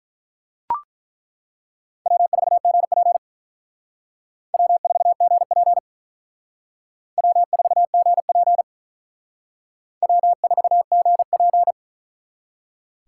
Morse code, heard as a keyed tone, about 42 wpm, 700 Hz. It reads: W4GP W4GP W4GP W4GP
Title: Speed Racing - USA 1x2 Call Signs 35wpm